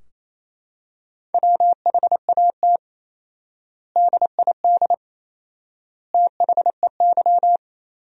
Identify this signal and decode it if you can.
Morse code, heard as a keyed tone, about 28 wpm, 700 Hz. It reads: WHAT DID THEY
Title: 4x Word-Spacing - Sentences from Top 100 Words - 28wpm